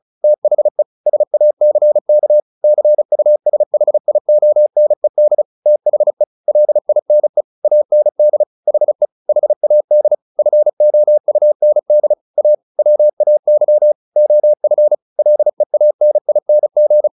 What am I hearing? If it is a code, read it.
Morse code, heard as a keyed tone, about 35 wpm, 600 Hz. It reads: THE SACK CUSHIONED THE LINE AND HE HAD FOUND A WAY OF LEANING
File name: the old man and the sea / the old man and the sea 1 35wpm